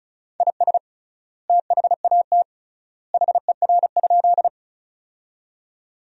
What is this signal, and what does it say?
Morse code, heard as a keyed tone, about 36 wpm, 700 Hz. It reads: IS THAT HER?
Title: 3x Word-Spacing - Sentences from Top 100 Words - 35wpm